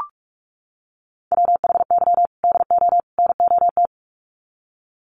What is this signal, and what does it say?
Morse code, heard as a keyed tone, about 44 wpm, 700 Hz. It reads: WHY DO NOT
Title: Sets of 3 Words 45wpm